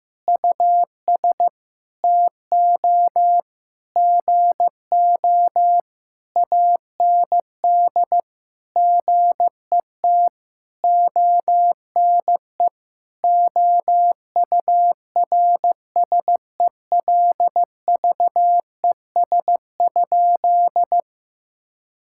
Morse code, heard as a keyed tone, about 15 wpm, 700 Hz. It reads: US TO GO AND GET ONE OURSELVES?
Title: Sentences from Top 1600 Words 15wpm